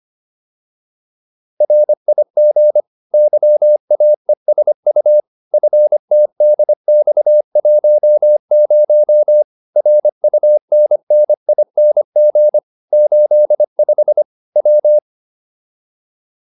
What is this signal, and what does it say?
Morse code, heard as a keyed tone, about 25 wpm, 600 Hz. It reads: RIG YAESU FTDX10 RUNNING 85 W